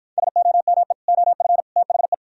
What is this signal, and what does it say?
Morse code, heard as a keyed tone, about 58 wpm, 700 Hz. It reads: SOME OF THE